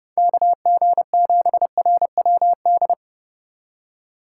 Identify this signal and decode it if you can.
Morse code, heard as a keyed tone, about 32 wpm, 700 Hz. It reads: KG7RWD
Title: USA Call Signs - 4x Character Spacing - 30wpm